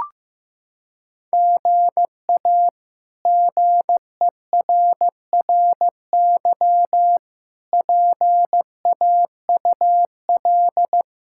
Morse code, heard as a keyed tone, about 15 wpm, 700 Hz. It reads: GA GERRY PAUL
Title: K1USN SST QSO Practice - 15wpm